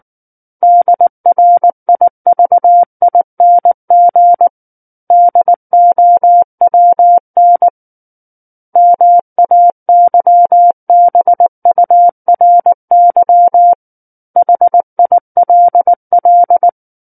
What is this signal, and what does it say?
Morse code, heard as a keyed tone, about 19 wpm, 700 Hz. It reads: DRIVING DOWN MAYBURY HILL
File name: Book One / 10-Chapter 10 - 19 WPM